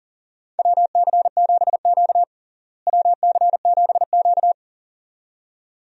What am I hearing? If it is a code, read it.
Morse code, heard as a keyed tone, about 40 wpm, 700 Hz. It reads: WC7Q WC7Q
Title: USA Call Signs - 4 Characters - Repeated 2x - 40wpm